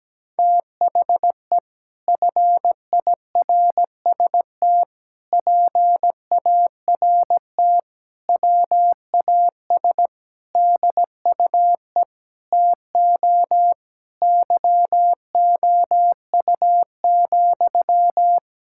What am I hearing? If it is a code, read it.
Morse code, heard as a keyed tone, about 17 wpm, 700 Hz. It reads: THE FIRST PART WAS DUE TO YOU,
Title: Sentences from Top 1000 Words - Review Part 6 - 17wpm